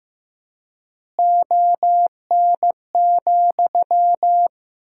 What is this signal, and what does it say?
Morse code, heard as a keyed tone, about 16 wpm, 700 Hz. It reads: ON,
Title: 4x Word-Spacing - Sentences from Top 100 Words - 15wpm